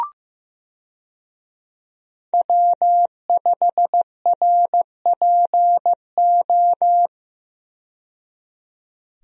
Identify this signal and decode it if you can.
Morse code, heard as a keyed tone, about 15 wpm, 700 Hz. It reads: W5RPO
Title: USA Call Signs - 5 Characters - 15wpm